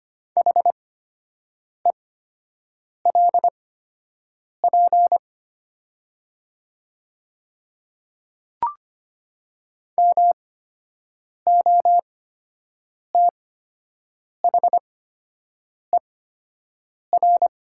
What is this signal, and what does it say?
Morse code, heard as a keyed tone, about 25 wpm, spaced as for 7 wpm, 700 Hz. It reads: HELP MOTHER
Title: 8x Character Spacing - Top 300 Words - 25wpm